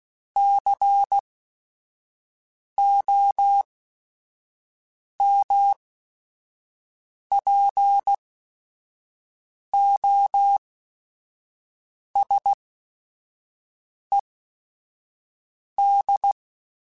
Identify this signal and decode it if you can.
Morse code, heard as a keyed tone, about 16 wpm, spaced as for 5 wpm, 800 Hz. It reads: COMPOSED
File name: Technical Subjects / Technical Subjects-mp3s / Sw Pwr Sup @5